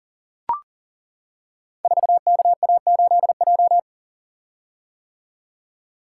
Morse code, heard as a keyed tone, about 38 wpm, 700 Hz. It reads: 4KA8J
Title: Code Groups - 5 Letter:Numbers - 40wpm